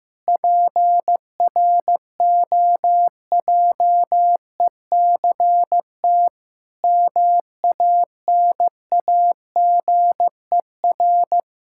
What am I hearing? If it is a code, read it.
Morse code, heard as a keyed tone, about 16 wpm, 700 Hz. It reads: PROJECT MANAGER
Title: Practice QSO - Exchange 3 - More Info - 15wpm